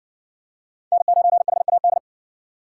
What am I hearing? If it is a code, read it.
Morse code, heard as a keyed tone, about 55 wpm, 700 Hz. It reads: N9HAD